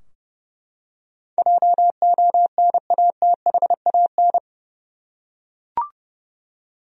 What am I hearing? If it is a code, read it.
Morse code, heard as a keyed tone, about 30 wpm, 700 Hz. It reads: JONATHAN
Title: USA Common Names - 30wpm